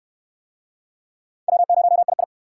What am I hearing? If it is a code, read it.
Morse code, heard as a keyed tone, about 64 wpm, 700 Hz. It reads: W0II